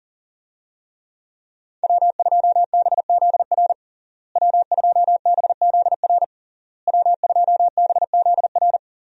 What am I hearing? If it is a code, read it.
Morse code, heard as a keyed tone, about 40 wpm, 700 Hz. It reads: W2BZR W2BZR W2BZR
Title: USA Call Signs - 5 Characters - Repeated 3x - 40wpm